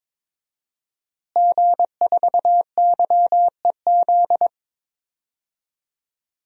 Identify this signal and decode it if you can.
Morse code, heard as a keyed tone, about 22 wpm, 700 Hz. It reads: G4YEZ
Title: DXCC-100 Call Signs - 5 Characters - Repeated 3x - 22wpm